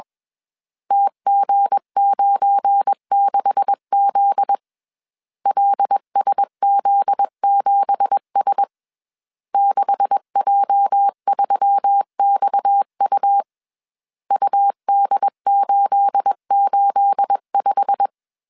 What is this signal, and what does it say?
Morse code, heard as a keyed tone, about 21 wpm, 800 Hz. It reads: TG96Z LSZ7S 6J3XU UD885